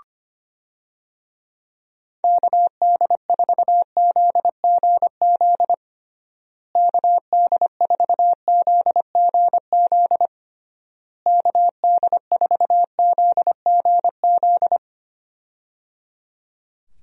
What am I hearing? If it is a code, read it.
Morse code, heard as a keyed tone, about 25 wpm, 700 Hz. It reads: KD4ZGZ KD4ZGZ KD4ZGZ
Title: USA Call Signs - 6 Characters - Repeated 3x - 25wpm